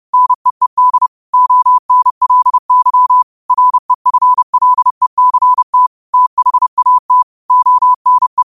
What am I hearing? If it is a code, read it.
Morse code, heard as a keyed tone, about 31 wpm, 1000 Hz. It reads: NEED ONLY REFLECT THAT ONE